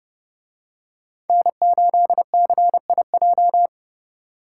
Morse code, heard as a keyed tone, about 29 wpm, 700 Hz. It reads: N8CIJ